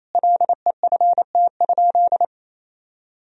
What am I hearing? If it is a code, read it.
Morse code, heard as a keyed tone, about 28 wpm, 700 Hz. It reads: LEFT?
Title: Sentences from Top 1000 Words - Review Part 2 - 28wpm